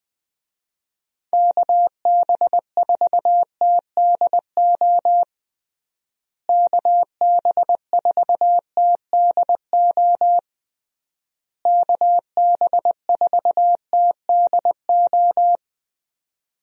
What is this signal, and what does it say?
Morse code, heard as a keyed tone, about 21 wpm, 700 Hz. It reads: KB4TDO KB4TDO KB4TDO
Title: USA Call Signs - 6 Characters - Repeated 3x - 20wpm